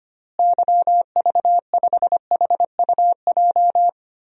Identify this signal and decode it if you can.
Morse code, heard as a keyed tone, about 25 wpm, 700 Hz. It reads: YV5HUJ
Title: DXCC-100 Call Signs - 6 Characters - Repeated 2x - 25wpm